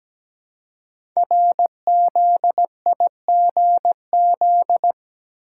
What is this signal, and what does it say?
Morse code, heard as a keyed tone, about 17 wpm, 700 Hz. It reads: RZIGZ
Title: Code Groups - 5 Letters - 17wpm